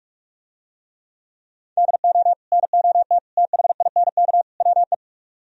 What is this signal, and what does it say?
Morse code, heard as a keyed tone, about 47 wpm, 700 Hz. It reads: DO NOT THINK WE